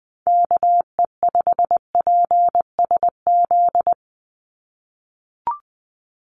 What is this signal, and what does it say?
Morse code, heard as a keyed tone, about 20 wpm, 700 Hz. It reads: KE5PSZ
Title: USA Call Signs 20wpm